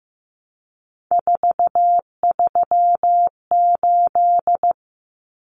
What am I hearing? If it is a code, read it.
Morse code, heard as a keyed tone, about 15 wpm, 700 Hz. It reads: 438
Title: Numbers - 3-Digit 15wpm